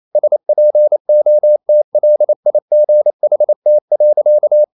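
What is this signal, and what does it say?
Morse code, heard as a keyed tone, about 28 wpm, 600 Hz. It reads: SPOTLIGHT.
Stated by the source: Morse code practice at 28 wpm, text of Ken Follett's novel Never in English, from CW med Gnister.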